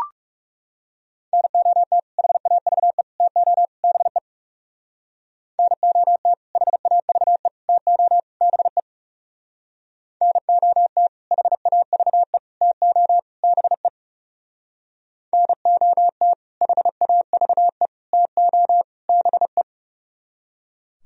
Morse code, heard as a keyed tone, about 38 wpm, 700 Hz. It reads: NOT HAVE TO BE NOT HAVE TO BE NOT HAVE TO BE NOT HAVE TO BE